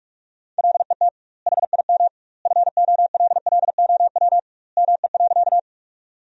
Morse code, heard as a keyed tone, about 41 wpm, 700 Hz. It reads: LET HIM FOLLOW ME.
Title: 2x Word-Spacing - Sentences from Top 300 Words - 45wpm